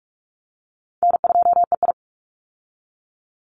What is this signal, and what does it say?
Morse code, heard as a keyed tone, about 44 wpm, 700 Hz. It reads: N2EI
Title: Easy Call Signs Repeated 2x - 45wpm